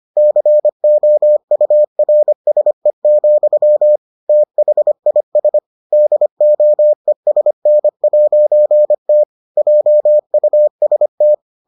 Morse code, heard as a keyed tone, about 25 wpm, 600 Hz. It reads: COURSE, THIS DOESN'T JUST